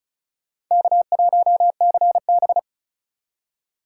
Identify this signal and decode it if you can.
Morse code, heard as a keyed tone, about 45 wpm, 700 Hz. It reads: K1CB